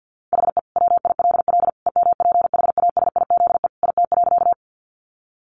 Morse code, heard as a keyed tone, about 50 wpm, 700 Hz. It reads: HE WILL EMPHASIZE IT.
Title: Sentences from Top 1500 Words 50wpm